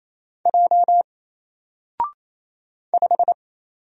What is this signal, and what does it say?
Morse code, heard as a keyed tone, about 27 wpm, 700 Hz. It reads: J 5
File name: Single-Letter-Number-28wpm